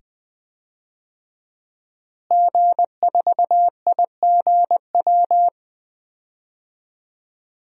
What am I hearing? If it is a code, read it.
Morse code, heard as a keyed tone, about 20 wpm, 700 Hz. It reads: G4IGW
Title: DXCC-100 Call Signs - 5 Characters - Repeated 2x - 20wpm